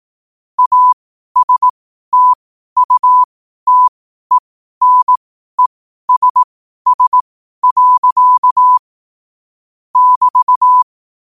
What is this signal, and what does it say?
Morse code, heard as a keyed tone, about 18 wpm, spaced as for 12 wpm, 1000 Hz. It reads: ASTUTENESS. <BT>
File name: qotdfiles / 12wpm